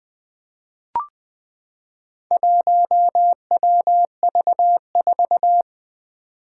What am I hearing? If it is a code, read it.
Morse code, heard as a keyed tone, about 20 wpm, 700 Hz. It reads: 1WV4
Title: Four Challenging Letter-Number Combinations 20wpm